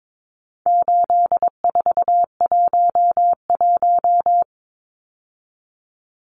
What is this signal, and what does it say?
Morse code, heard as a keyed tone, about 22 wpm, 700 Hz. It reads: 8411
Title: Numbers - 4-Digit 22wpm